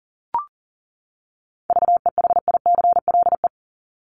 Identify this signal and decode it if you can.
Morse code, heard as a keyed tone, about 40 wpm, 700 Hz. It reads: VEHICLE
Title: Top 1751-2000 Words 40wpm